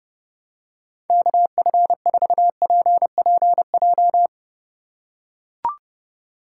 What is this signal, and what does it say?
Morse code, heard as a keyed tone, about 30 wpm, 700 Hz. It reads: KF4PPJ
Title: USA Call Signs - 3x Character Spacing - 30wpm